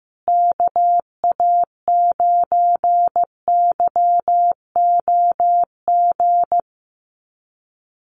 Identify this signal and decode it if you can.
Morse code, heard as a keyed tone, about 14 wpm, 700 Hz. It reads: KA9YOG
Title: USA Call Signs Repeated 2x 15wpm